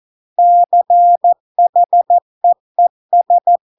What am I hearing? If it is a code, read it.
Morse code, heard as a keyed tone, about 14 wpm, 700 Hz. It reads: CHEES